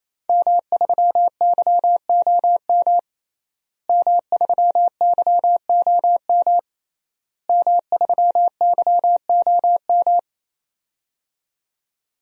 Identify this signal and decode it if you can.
Morse code, heard as a keyed tone, about 28 wpm, 700 Hz. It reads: M3YOM M3YOM M3YOM
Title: DXCC-100 Call Signs - 5 Characters - Repeated 3x - 28wpm